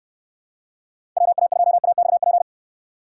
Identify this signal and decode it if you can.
Morse code, heard as a keyed tone, about 62 wpm, 700 Hz. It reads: WA2AXP